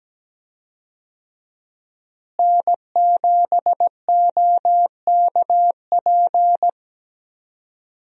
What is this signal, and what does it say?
Morse code, heard as a keyed tone, about 17 wpm, 700 Hz. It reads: N7OKP
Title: USA Call Signs - 5 Characters - Repeated 2x - 17wpm